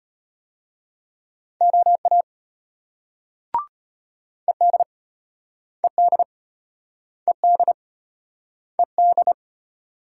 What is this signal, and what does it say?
Morse code, heard as a keyed tone, about 35 wpm, 700 Hz. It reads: OA ED ED ED ED